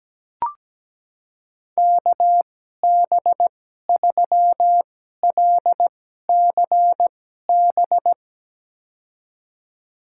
Morse code, heard as a keyed tone, about 17 wpm, spaced as for 12 wpm, 700 Hz. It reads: KB3LCB
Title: USA Call Signs - 6 Characters - 2x Character Spacing - 17wpm